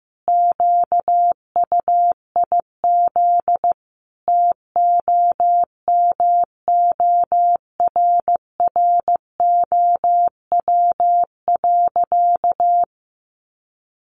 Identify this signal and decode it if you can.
Morse code, heard as a keyed tone, about 15 wpm, 700 Hz. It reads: QUIZ TOMORROW.
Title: Additional Easy Phrases 15wpm